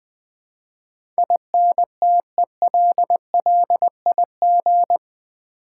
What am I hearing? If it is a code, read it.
Morse code, heard as a keyed tone, about 20 wpm, 700 Hz. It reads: INTELLIG
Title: Build A Word - 4 Letters At A Time - 11-Letter Words - 20wpm